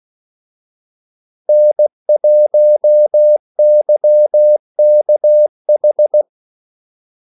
Morse code, heard as a keyed tone, about 16 wpm, 600 Hz. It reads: N1YKH
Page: Kaldesignaler 16 wpm